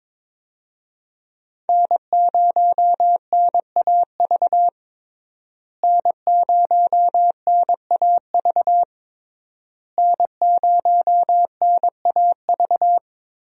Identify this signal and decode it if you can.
Morse code, heard as a keyed tone, about 22 wpm, 700 Hz. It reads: N0NAV N0NAV N0NAV
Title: USA Call Signs - 5 Characters - Repeated 3x - 22wpm